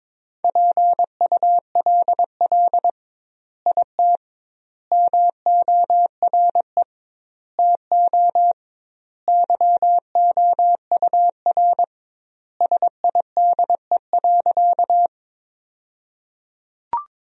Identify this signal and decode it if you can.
Morse code, heard as a keyed tone, about 22 wpm, 700 Hz. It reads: PULL IT MORE TO YOUR SIDE.
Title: 2x Word-Spacing - Sentences from Top 400 Words - 22wpm